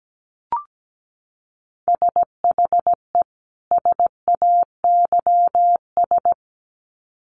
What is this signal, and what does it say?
Morse code, heard as a keyed tone, about 17 wpm, 700 Hz. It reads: SHE SAYS
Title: Sets of 2 Words 17wpm